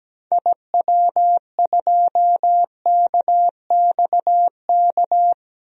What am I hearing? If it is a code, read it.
Morse code, heard as a keyed tone, about 17 wpm, 700 Hz. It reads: IW2KXK
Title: DXCC-100 Call Signs - 6 Characters - 17wpm